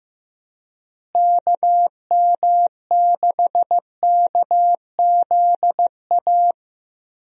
Morse code, heard as a keyed tone, about 15 wpm, 700 Hz. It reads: KM6KZA
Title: USA Call Signs - 6 Characters - Repeated 2x - 15wpm